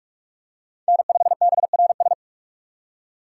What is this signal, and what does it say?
Morse code, heard as a keyed tone, about 47 wpm, 700 Hz. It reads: N5BRS
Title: USA Call Signs - 5 Characters - 3x Character Spacing - 45wpm